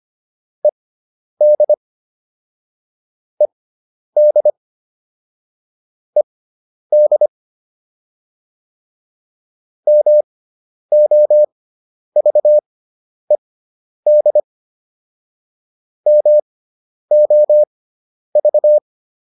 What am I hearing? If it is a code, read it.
Morse code, heard as a keyed tone, about 25 wpm, spaced as for 10 wpm, 600 Hz. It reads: ED ED ED MOVED MOV
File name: ED5 10